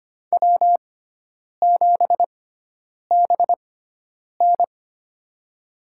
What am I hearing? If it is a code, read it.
Morse code, heard as a keyed tone, about 25 wpm, spaced as for 9 wpm, 700 Hz. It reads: W7BN
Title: USA Call Signs - 4 Characters - 6x Character Spacing - 25wpm